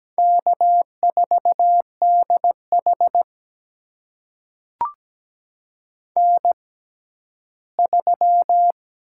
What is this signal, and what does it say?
Morse code, heard as a keyed tone, about 17 wpm, spaced as for 11 wpm, 700 Hz. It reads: K4DH N3
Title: USA Call Signs - 4 Characters - 6x Character Spacing - 17wpm